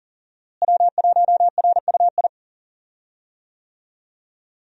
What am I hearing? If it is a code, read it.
Morse code, heard as a keyed tone, about 40 wpm, 700 Hz. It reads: W1RUI
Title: USA Call Signs - 5 Characters - Repeated 2x - 40wpm